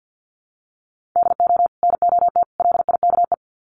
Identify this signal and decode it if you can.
Morse code, heard as a keyed tone, about 50 wpm, 700 Hz. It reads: DO NOT LIKE
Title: Sets of 3 Words 50wpm